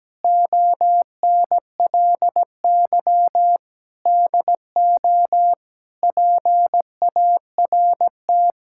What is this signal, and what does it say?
Morse code, heard as a keyed tone, about 17 wpm, 700 Hz. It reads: ONLY DO PART
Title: Sentences from Top 1000 Words - Review Part 5 - 17wpm